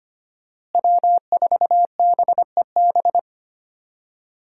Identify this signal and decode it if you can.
Morse code, heard as a keyed tone, about 25 wpm, 700 Hz. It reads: W4BEB
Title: USA Call Signs - 5 Characters - Repeated 2x - 25wpm